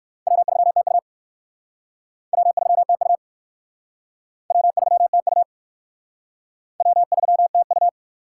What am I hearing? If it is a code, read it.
Morse code, heard as a keyed tone, about 54 wpm, 700 Hz. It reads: W3TU W3TU W3TU W3TU
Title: Speed Racing - USA 1x2 Call Signs 45wpm